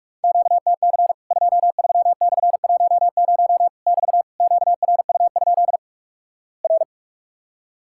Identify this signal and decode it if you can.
Morse code, heard as a keyed tone, about 46 wpm, 700 Hz. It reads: QTC 23/10 <BT> QRU? R
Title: Worked All Europe DX Contest - QTC Exchange - 45wpm